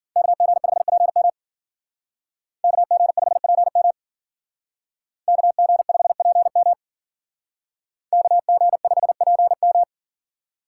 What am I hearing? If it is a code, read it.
Morse code, heard as a keyed tone, about 48 wpm, 700 Hz. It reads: KG5PM KG5PM KG5PM KG5PM
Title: Speed Racing - USA 2x2 Call Signs 40wpm